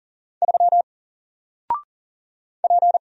Morse code, heard as a keyed tone, about 40 wpm, 700 Hz. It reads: 3 P
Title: Single Character (Letter:Number:Punctuation:Prosign) 40wpm